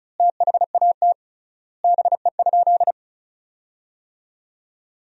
Morse code, heard as a keyed tone, about 36 wpm, 700 Hz. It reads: THAT BE?
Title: 3x Word-Spacing - Sentences from Top 100 Words - 35wpm